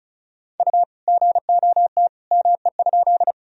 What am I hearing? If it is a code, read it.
Morse code, heard as a keyed tone, about 36 wpm, 700 Hz. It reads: U GOT ME?
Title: Sentences from Top 300 Words and Abbreviations 35wpm